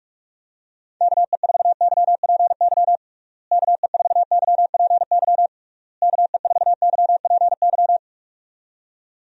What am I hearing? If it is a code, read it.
Morse code, heard as a keyed tone, about 47 wpm, 700 Hz. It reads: KE4YPY KE4YPY KE4YPY
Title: USA Call Signs - 6 Characters - Repeated 3x - 45wpm